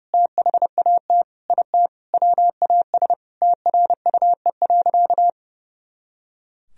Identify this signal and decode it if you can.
Morse code, heard as a keyed tone, about 30 wpm, 700 Hz. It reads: THAT IT WAS TRUE.
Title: Sentences from Top 400 Words and Abbreviations 30wpm